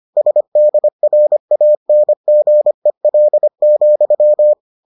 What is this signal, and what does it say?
Morse code, heard as a keyed tone, about 25 wpm, 600 Hz. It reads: SDRANGEL,